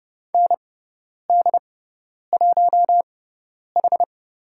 Morse code, heard as a keyed tone, about 30 wpm, spaced as for 10 wpm, 700 Hz. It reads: ND1H
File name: Lesson 17g - Letter u - Call Signs 30wpm